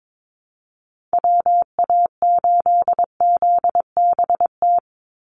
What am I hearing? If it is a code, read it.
Morse code, heard as a keyed tone, about 22 wpm, 700 Hz. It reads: WA8ZBT
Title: Call Signs Repeated 3x - 22wpm